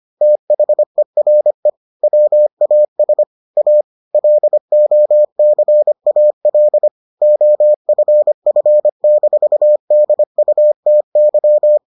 Morse code, heard as a keyed tone, about 25 wpm, 600 Hz. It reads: THERE WAS A LOCAL OFF-DUTY